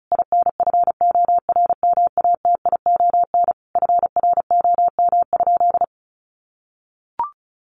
Morse code, heard as a keyed tone, about 34 wpm, 700 Hz. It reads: INFORMATION FROM?
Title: Sentences from Top 600 Words 35wpm